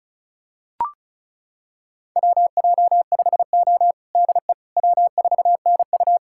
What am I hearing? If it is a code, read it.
Morse code, heard as a keyed tone, about 36 wpm, 700 Hz. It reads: WJ5O DE W4NU
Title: USA Callsign DE Callsign - 4 Characters - 35wpm